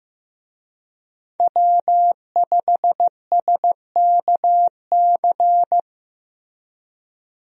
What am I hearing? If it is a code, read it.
Morse code, heard as a keyed tone, about 16 wpm, 700 Hz. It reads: W5SKC
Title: USA Call Signs - 5 Characters - Repeated 2x - 15wpm